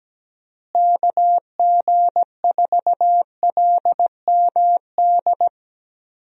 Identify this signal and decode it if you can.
Morse code, heard as a keyed tone, about 17 wpm, 700 Hz. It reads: KG4LMD
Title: USA Call Signs - 6 Characters - Repeated 2x - 17wpm